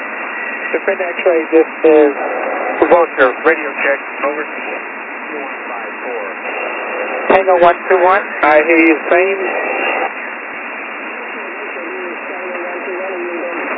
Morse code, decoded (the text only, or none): none